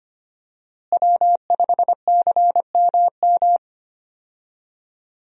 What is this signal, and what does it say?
Morse code, heard as a keyed tone, about 25 wpm, 700 Hz. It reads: W5CMM
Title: USA Call Signs - 5 Characters - Repeated 2x - 25wpm